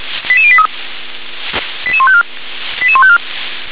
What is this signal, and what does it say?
Five-tone selective call (ZVEI), heard as quick runs of five tones, tones of about 70 ms. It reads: 88083 88154 88154